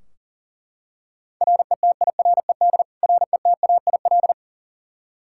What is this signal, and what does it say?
Morse code, heard as a keyed tone, about 40 wpm, 700 Hz. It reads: RETIRED RETAIL
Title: K1USN SST QSO Practice 40wpm